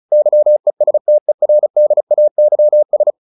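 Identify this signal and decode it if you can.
Morse code, heard as a keyed tone, about 37 wpm, 600 Hz. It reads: YESTERDAYS
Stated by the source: Morse practice at 35 wpm from CW med Gnister, sending Ken Follett's novel Never in English.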